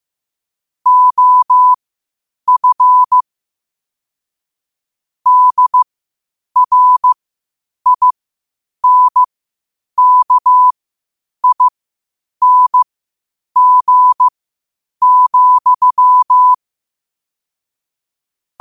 Morse code, heard as a keyed tone, about 15 wpm, spaced as for 8 wpm, 1000 Hz. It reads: OF DRINKING,
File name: qotdfiles / 8wpm